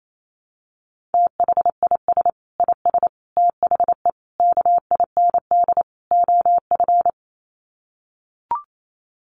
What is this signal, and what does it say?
Morse code, heard as a keyed tone, about 28 wpm, 700 Hz. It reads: THIS IS THE KIND OF
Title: Sets of 5 Words Encore 28wpm